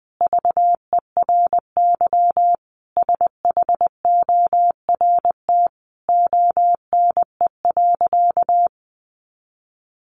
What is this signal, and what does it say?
Morse code, heard as a keyed tone, about 20 wpm, 700 Hz. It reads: VERY SHORT ONE.
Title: Sentences from Top 1100 Words 20wpm